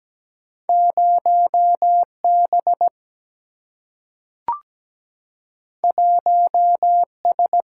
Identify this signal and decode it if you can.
Morse code, heard as a keyed tone, about 17 wpm, 700 Hz. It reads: 0 B 1 S
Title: Single-Letter Single-Number 17wpm